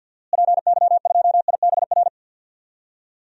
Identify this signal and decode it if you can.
Morse code, heard as a keyed tone, about 53 wpm, 700 Hz. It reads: PY2IBR
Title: DXCC-100 Call Signs - 6 Characters - Repeated 2x - 50wpm